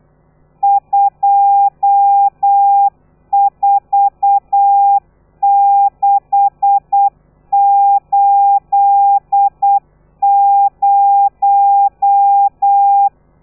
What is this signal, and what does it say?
Morse code, heard as a keyed tone, about 8 wpm, 800 Hz. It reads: 24680
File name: telegrafia / i2tfj / Lez13 35